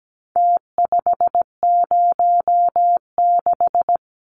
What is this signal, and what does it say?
Morse code, heard as a keyed tone, about 17 wpm, 700 Hz. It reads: T506
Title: Numbers - Contest Style - 4-Digit 17wpm